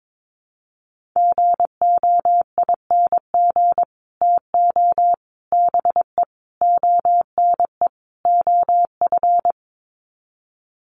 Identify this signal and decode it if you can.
Morse code, heard as a keyed tone, about 22 wpm, 700 Hz. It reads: GOING TO BE ONE OF
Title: Sets of 5 Words Encore2 22wpm